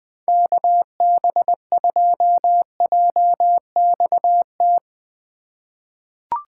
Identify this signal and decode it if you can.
Morse code, heard as a keyed tone, about 20 wpm, 700 Hz. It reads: KB2JXT
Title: USA Call Signs - 6 Characters - 2x Character Spacing - 20wpm